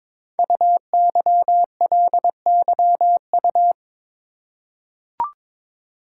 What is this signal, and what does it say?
Morse code, heard as a keyed tone, about 22 wpm, 700 Hz. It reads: UYLYU
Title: Code Groups - 5 Letters - 22wpm